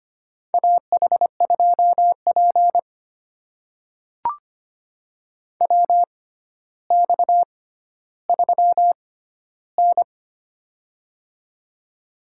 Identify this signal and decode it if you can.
Morse code, heard as a keyed tone, about 25 wpm, spaced as for 13 wpm, 700 Hz. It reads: AH2P WX3N
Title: USA Call Signs - 6x Character Spacing - 25wpm